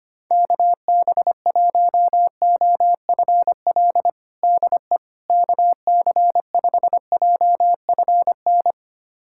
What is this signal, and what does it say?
Morse code, heard as a keyed tone, about 25 wpm, 700 Hz. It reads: KB1OFL DE KC5JFN
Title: USA Callsign DE Callsign - 6 Characters - 25wpm